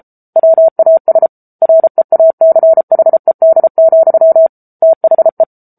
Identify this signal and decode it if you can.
Morse code, heard as a keyed tone, about 34 wpm, 650 Hz. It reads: WAS REACHED, THE
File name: Book Two / 07-Chapter 7 - 33 WPM